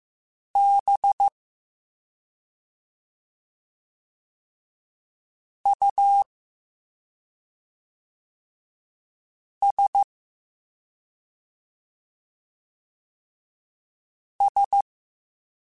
Morse code, heard as a keyed tone, about 15 wpm, 800 Hz. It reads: B U S S